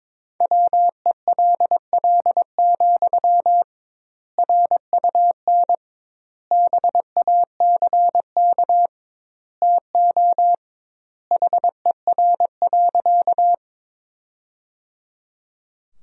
Morse code, heard as a keyed tone, about 22 wpm, 700 Hz. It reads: WELL, RUN BACK TO HER.
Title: 2x Word-Spacing - Sentences from Top 300 Words - 22wpm